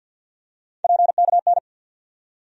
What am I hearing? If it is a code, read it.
Morse code, heard as a keyed tone, about 47 wpm, 700 Hz. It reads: PKN
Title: USA Call Sign Postfixes - 3 Letters 50wpm